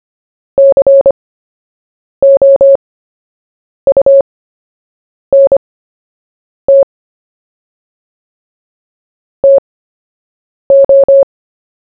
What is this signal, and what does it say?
Morse code, heard as a keyed tone, about 25 wpm, spaced as for 7 wpm, 550 Hz. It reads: COUNT TO
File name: cw / fists / Short-Story-1-7wpm